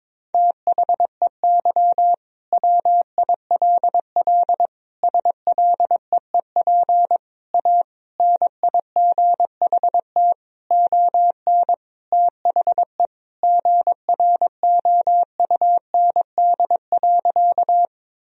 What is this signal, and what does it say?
Morse code, heard as a keyed tone, about 22 wpm, 700 Hz. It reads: THEY WILL SLEEP A NIGHT ON THE GROUND.
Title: Sentences from Top 1000 Words - Review Part 2 - 22wpm